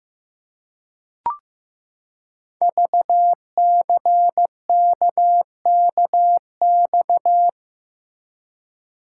Morse code, heard as a keyed tone, about 15 wpm, 700 Hz. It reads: VCKKX
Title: Code Groups - 5 Letters - 15wpm